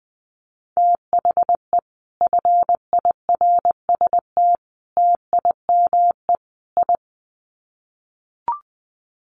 Morse code, heard as a keyed tone, about 20 wpm, 700 Hz. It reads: THE FIRST TIME I
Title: Sets of 4 Words 20wpm